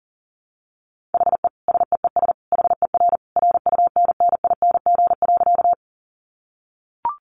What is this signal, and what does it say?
Morse code, heard as a keyed tone, about 40 wpm, 700 Hz. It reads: HE SEES HER RUNNING.
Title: Additional Easy Phrases 40wpm